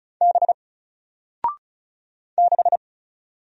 Morse code, heard as a keyed tone, about 36 wpm, 700 Hz. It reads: B 6
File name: Single-Letter-Number-35wpm